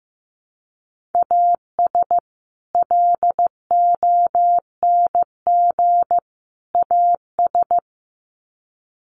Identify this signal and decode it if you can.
Morse code, heard as a keyed tone, about 15 wpm, 700 Hz. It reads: AS LONG AS
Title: Sets of 3 Words 15wpm